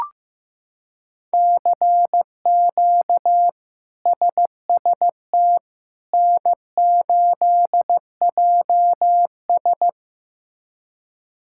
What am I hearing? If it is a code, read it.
Morse code, heard as a keyed tone, about 15 wpm, 700 Hz. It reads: CQ SST N8JS
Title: K1USN SST QSO Practice - 15wpm